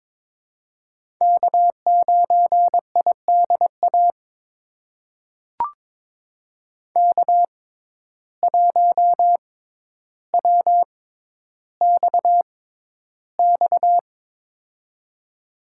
Morse code, heard as a keyed tone, about 22 wpm, spaced as for 11 wpm, 700 Hz. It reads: K9IDA K1WXX
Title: USA Call Signs - 5 Characters - 6x Character Spacing - 22wpm